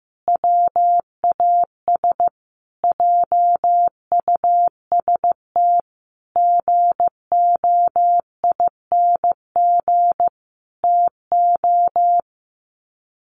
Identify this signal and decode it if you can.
Morse code, heard as a keyed tone, about 15 wpm, 700 Hz. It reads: WAS JUST GOING TO
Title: Sets of 5 Words Encore 15wpm